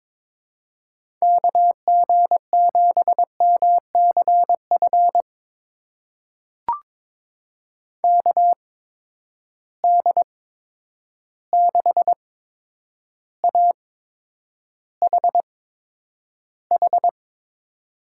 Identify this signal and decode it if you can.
Morse code, heard as a keyed tone, about 22 wpm, spaced as for 9 wpm, 700 Hz. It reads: KG7MCF KD6AHH
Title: USA Call Signs - 6 Characters - 8x Character Spacing - 22wpm